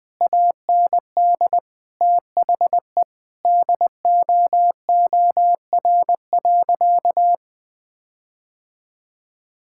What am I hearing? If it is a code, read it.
Morse code, heard as a keyed tone, about 21 wpm, 700 Hz. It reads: AND THE DOOR.
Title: Sentences from Top 1600 Words 20wpm